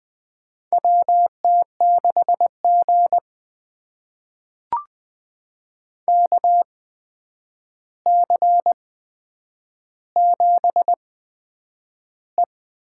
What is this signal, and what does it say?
Morse code, heard as a keyed tone, about 20 wpm, spaced as for 9 wpm, 700 Hz. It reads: WT6G KC7E
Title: USA Call Signs - 4 Characters - 8x Character Spacing - 20wpm